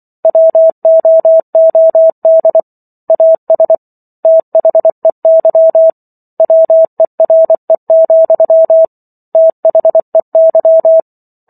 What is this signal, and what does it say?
Morse code, heard as a keyed tone, about 24 wpm, 650 Hz. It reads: WOOD AS THEY WERE, THEY